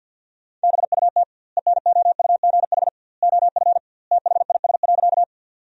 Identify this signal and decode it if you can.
Morse code, heard as a keyed tone, about 50 wpm, 700 Hz. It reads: BUT ENOUGH OF THIS.
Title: 2x Word-Spacing - Sentences from Top 400 Words - 50wpm